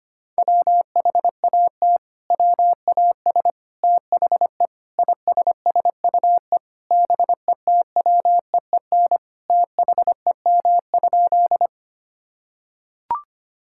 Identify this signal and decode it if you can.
Morse code, heard as a keyed tone, about 25 wpm, 700 Hz. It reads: WHAT WAS THE ISSUE BETWEEN THEM?
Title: Sentences from Top 1000 Words - Review Part 6 - 25wpm